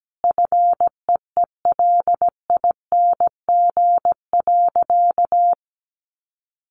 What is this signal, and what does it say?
Morse code, heard as a keyed tone, about 17 wpm, 700 Hz. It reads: FEELING.
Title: Sentences from Top 1400 Words 17wpm